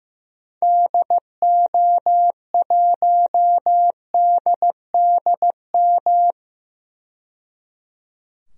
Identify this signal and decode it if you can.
Morse code, heard as a keyed tone, about 16 wpm, 700 Hz. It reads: DO1DDM
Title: DXCC-100 Call Signs - 6 Characters - 15wpm